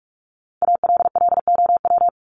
Morse code, heard as a keyed tone, about 44 wpm, 700 Hz. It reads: ALLOW
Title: ED Suffix - Top 100 Words with Base Words 45wpm